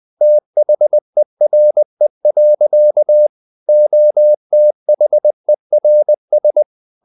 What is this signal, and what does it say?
Morse code, heard as a keyed tone, about 20 wpm, 600 Hz. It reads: THERE. OTHERS
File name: the old man and the sea / the old man and the sea part 1 20wpm